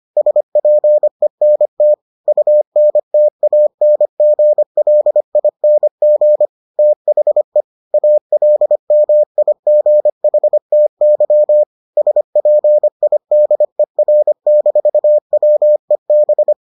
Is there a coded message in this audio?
Yes: Morse code from 0.2 to 16.5 s, heard as a keyed tone, about 25 wpm, 600 Hz. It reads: SPENT UNTANGLING THE ALMIGHTY SPIDER-WEB